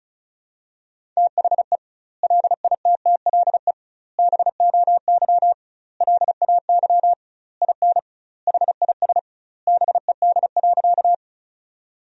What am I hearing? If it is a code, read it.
Morse code, heard as a keyed tone, about 36 wpm, 700 Hz. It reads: THE LITTLE BOY LAY IN HIS BED.
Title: 2x Word-Spacing - Sentences from Top 500 Words - 35wpm